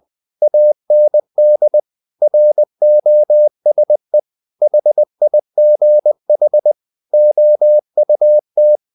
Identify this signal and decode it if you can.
Morse code, heard as a keyed tone, about 20 wpm, 600 Hz. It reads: AND ROSE HIGH OUT
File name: the old man and the sea part 2 20wpm